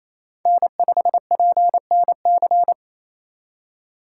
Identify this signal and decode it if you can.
Morse code, heard as a keyed tone, about 28 wpm, 700 Hz. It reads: N5PNC
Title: USA Call Signs - 5 Characters - 4x Character Spacing - 28wpm